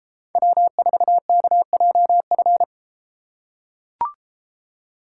Morse code, heard as a keyed tone, about 32 wpm, 700 Hz. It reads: W4KJF